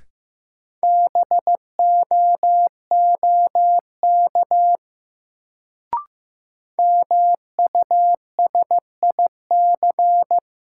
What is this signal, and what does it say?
Morse code, heard as a keyed tone, about 15 wpm, 700 Hz. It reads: BOOK MUSIC
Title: Sentences from Top 300 Words - New Word Drill - 15wpm